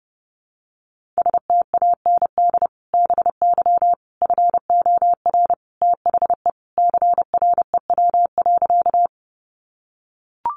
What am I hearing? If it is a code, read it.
Morse code, heard as a keyed tone, about 30 wpm, 700 Hz. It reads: STAND BY FOR THE CREW.
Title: Sentences from Top 1500 Words 30wpm